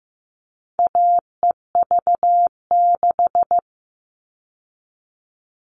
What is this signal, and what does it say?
Morse code, heard as a keyed tone, about 15 wpm, 700 Hz. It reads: AEV6
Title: All Cut Numbers - 4-Digit 15wpm